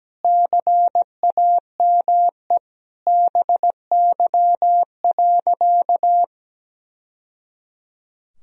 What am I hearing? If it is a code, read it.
Morse code, heard as a keyed tone, about 17 wpm, 700 Hz. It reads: CAME BY.
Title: Sentences from Top 1000 Words - Review Part 1 - 17wpm